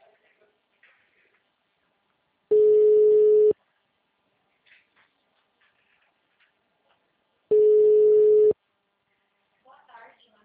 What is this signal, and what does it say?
Ringback tone (Europe, 425 Hz).
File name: out- -2006- - - .863